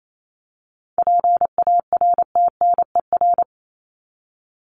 Morse code, heard as a keyed tone, about 28 wpm, 700 Hz. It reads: PARTNER
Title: Top 1501-1750 Words 28wpm